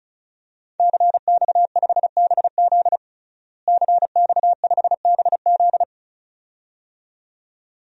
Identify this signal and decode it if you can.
Morse code, heard as a keyed tone, about 36 wpm, 700 Hz. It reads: CX5BZ CX5BZ
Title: DXCC-100 Call Signs - 5 Characters - Repeated 2x - 35wpm